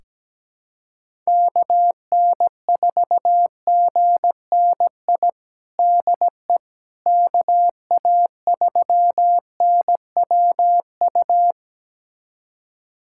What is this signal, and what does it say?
Morse code, heard as a keyed tone, about 17 wpm, 700 Hz. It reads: KN4GNI DE KA3NWU
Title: USA Callsign DE Callsign - 6 Characters - 17wpm